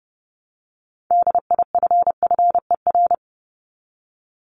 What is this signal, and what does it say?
Morse code, heard as a keyed tone, about 30 wpm, 700 Hz. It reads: DIFFER
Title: Top 500 Words 30wpm